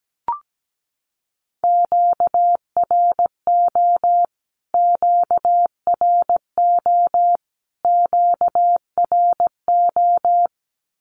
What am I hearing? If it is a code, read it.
Morse code, heard as a keyed tone, about 17 wpm, 700 Hz. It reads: QRO QRO QRO
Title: Q-Codes Repeated 3x 17wpm